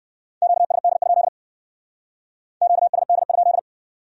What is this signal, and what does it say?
Morse code, heard as a keyed tone, about 55 wpm, 700 Hz. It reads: QSD? QSD?
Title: Speed-Racing Q-Codes - 45wpm